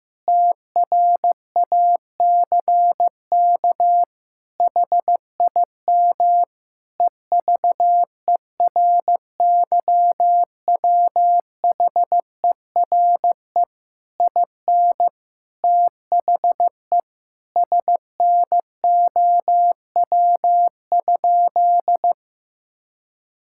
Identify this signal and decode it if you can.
Morse code, heard as a keyed tone, about 15 wpm, 700 Hz. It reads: TRACK HIM EVERYWHERE IN THE SNOW?
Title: Sentences from Top 1600 Words 15wpm